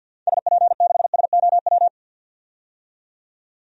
Morse code, heard as a keyed tone, about 47 wpm, 700 Hz. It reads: SP6SOW
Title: DXCC-100 Call Signs - 6 Characters - 50wpm